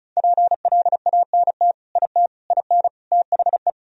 Morse code, heard as a keyed tone, about 36 wpm, 700 Hz. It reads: PLANT IT IN THE
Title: Sentences from Top 1000 Words - Review Part 1 - 35wpm